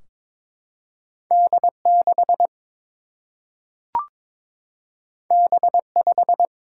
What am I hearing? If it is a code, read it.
Morse code, heard as a keyed tone, about 22 wpm, 700 Hz. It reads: D 6 B 5
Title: Single-Letter Single-Number 22wpm